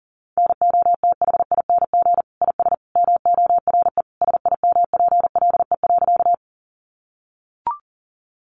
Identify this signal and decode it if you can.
Morse code, heard as a keyed tone, about 40 wpm, 700 Hz. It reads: NOTHING IS MORE SIMPLE.
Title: Sentences from Top 500 Words 40wpm